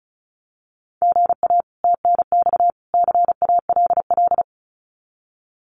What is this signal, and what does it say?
Morse code, heard as a keyed tone, about 34 wpm, 700 Hz. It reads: GA TNX CALL
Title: Common QSO Phrases 35wpm